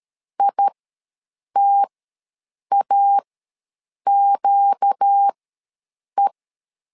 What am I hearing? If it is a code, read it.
Morse code, heard as a keyed tone, about 13 wpm, spaced as for 7 wpm, 800 Hz. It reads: ITAQE
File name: LEZIONI / 4 Q ws 12wpm 1912